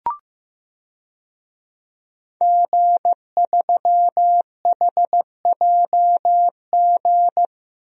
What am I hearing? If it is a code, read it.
Morse code, heard as a keyed tone, about 15 wpm, 700 Hz. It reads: G3HJG
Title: DXCC-100 Call Signs - 5 Characters - 15wpm